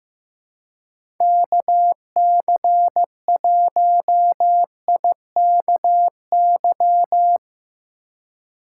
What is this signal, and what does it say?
Morse code, heard as a keyed tone, about 15 wpm, 700 Hz. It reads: KC1IKY
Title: USA Call Signs - 2x Character Spacing - 15wpm